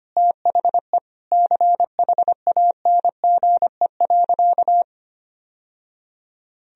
Morse code, heard as a keyed tone, about 25 wpm, 700 Hz. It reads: THE CHANGE.
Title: Sentences from Top 1000 Words - Review Part 1 - 25wpm